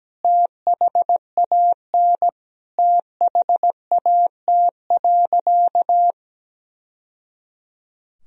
Morse code, heard as a keyed tone, about 17 wpm, 700 Hz. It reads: THAN THAT.
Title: Sentences from Top 1000 Words - Review Part 3 - 17wpm